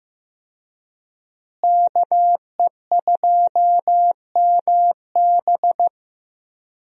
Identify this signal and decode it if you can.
Morse code, heard as a keyed tone, about 16 wpm, 700 Hz. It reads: KE2MB
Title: USA Call Signs - 5 Characters - Repeated 2x - 15wpm